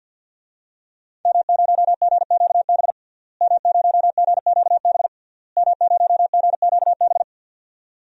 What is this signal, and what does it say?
Morse code, heard as a keyed tone, about 48 wpm, 700 Hz. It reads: M0GQB M0GQB M0GQB
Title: DXCC-100 Call Signs - 5 Characters - Repeated 3x - 50wpm